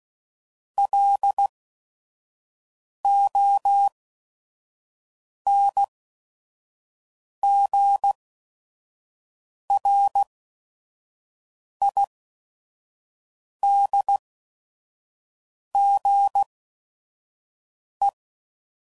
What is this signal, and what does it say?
Morse code, heard as a keyed tone, about 16 wpm, spaced as for 5 wpm, 800 Hz. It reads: LONGRIDGE
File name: CW Code Groups @5WPM / fromSM's / Cities